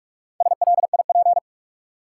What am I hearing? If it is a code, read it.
Morse code, heard as a keyed tone, about 46 wpm, 700 Hz. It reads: SLIP
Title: Four Letter Words - 45wpm